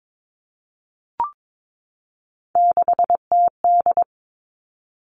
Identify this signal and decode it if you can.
Morse code, heard as a keyed tone, about 22 wpm, 700 Hz. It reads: 6TD
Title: Three Challenging Letter-Number Combinations 22wpm